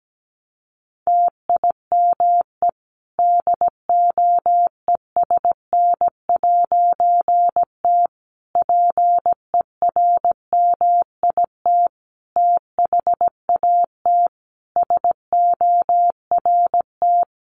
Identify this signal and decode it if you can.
Morse code, heard as a keyed tone, about 17 wpm, 700 Hz. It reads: TIME DOESN'T PERMIT THAT SORT
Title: A Sound Of Thunder 17wpm